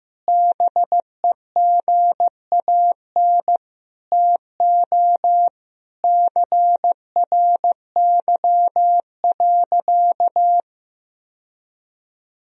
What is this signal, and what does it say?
Morse code, heard as a keyed tone, about 15 wpm, 700 Hz. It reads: BEGAN TO CRY.
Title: Sentences from Top 1000 Words - Review Part 2 - 15wpm